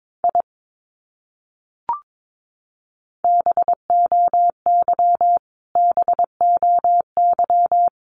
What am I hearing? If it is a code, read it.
Morse code, heard as a keyed tone, about 22 wpm, 700 Hz. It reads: I BOY BOY
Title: Top 200 Words Repeated 22wpm